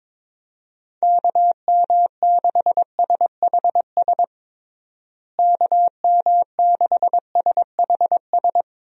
Morse code, heard as a keyed tone, about 22 wpm, 700 Hz. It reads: KM6SHS KM6SHS
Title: USA Call Signs - 6 Characters - Repeated 2x - 22wpm